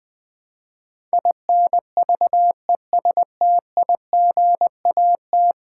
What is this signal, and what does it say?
Morse code, heard as a keyed tone, about 20 wpm, 700 Hz. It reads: INVESTIGAT
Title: Build A Word - 5 Letters At A Time - 12-Letter Words - 20wpm